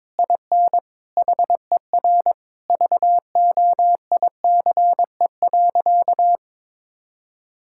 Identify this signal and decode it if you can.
Morse code, heard as a keyed tone, about 22 wpm, 700 Hz. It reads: IN HER VOICE.
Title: Sentences from Top 1000 Words - Review Part 6 - 22wpm